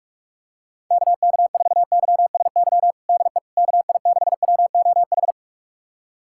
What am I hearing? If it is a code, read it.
Morse code, heard as a keyed tone, about 46 wpm, 700 Hz. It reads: KK4YSY DE KI6WOH
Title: USA Callsign DE Callsign 45wpm